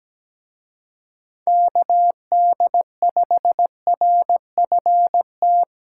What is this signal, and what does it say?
Morse code, heard as a keyed tone, about 20 wpm, 700 Hz. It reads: KD5RFT
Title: Speed Racing - USA 2x3 Call Signs 17wpm